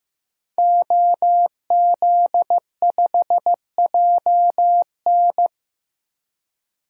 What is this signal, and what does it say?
Morse code, heard as a keyed tone, about 15 wpm, 700 Hz. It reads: OZ5JN
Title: DXCC-100 Call Signs - 5 Characters - Repeated 3x - 15wpm